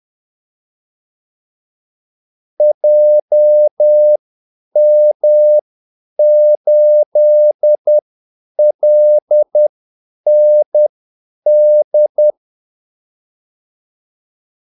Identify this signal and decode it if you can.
Morse code, heard as a keyed tone, about 10 wpm, spaced as for 8 wpm, 600 Hz. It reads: JM8LND
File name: Callsigns-10-8wpm